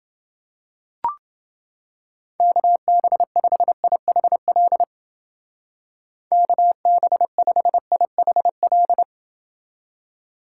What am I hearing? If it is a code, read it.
Morse code, heard as a keyed tone, about 31 wpm, 700 Hz. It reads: KB5IHL KB5IHL